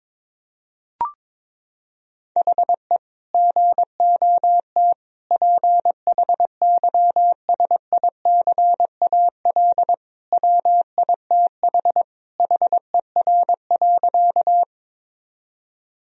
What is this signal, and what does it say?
Morse code, heard as a keyed tone, about 21 wpm, 700 Hz. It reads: HE GOT PHYSICAL WITH HER.
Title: Sentences from Top 1000 Words - Review Part 4 - 22wpm